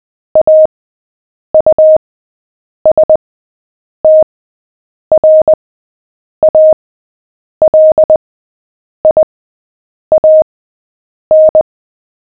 Morse code, heard as a keyed tone, about 20 wpm, spaced as for 8 wpm, 620 Hz. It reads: AUSTRALIAN
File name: cw / fists / Short-Story-7-8wpm